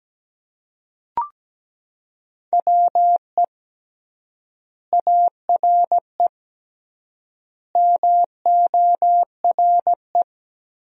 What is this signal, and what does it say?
Morse code, heard as a keyed tone, about 17 wpm, 700 Hz. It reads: WE ARE MORE
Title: 3x Word-Spacing - Sentences from Top 500 Words - 17wpm